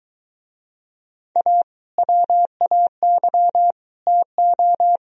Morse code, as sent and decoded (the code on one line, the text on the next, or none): .- / .-- .- -.-- / - ---
A WAY TO